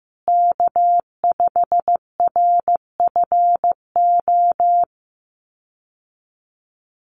Morse code, heard as a keyed tone, about 15 wpm, 700 Hz. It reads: K5RFO
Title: USA Call Signs Repeated 2x 15wpm